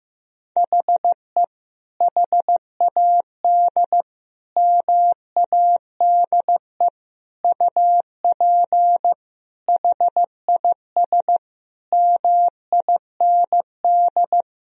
Morse code, heard as a keyed tone, about 15 wpm, 700 Hz. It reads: HE HAD MADE UP HIS MIND
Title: Sentences from Top 1000 Words - Review Part 3 - 15wpm